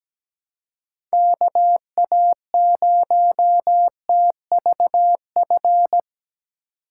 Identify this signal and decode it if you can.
Morse code, heard as a keyed tone, about 17 wpm, 700 Hz. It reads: KA0TVF
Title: USA Call Signs - 6 Characters - 4x Character Spacing - 17wpm